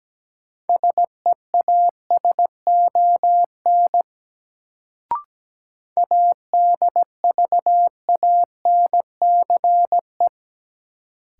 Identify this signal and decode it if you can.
Morse code, heard as a keyed tone, about 17 wpm, 700 Hz. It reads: SEASON ADVANCE
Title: Sentences from Top 1400 Words - New Word Drill - 17wpm